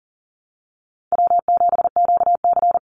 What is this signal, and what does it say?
Morse code, heard as a keyed tone, about 40 wpm, 700 Hz. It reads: W7QC
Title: Call Signs Repeated 3x - 40wpm